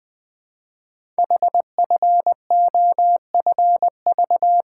Morse code, heard as a keyed tone, about 20 wpm, 700 Hz. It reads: HFOFV
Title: Code Groups - 5 Letters - 20wpm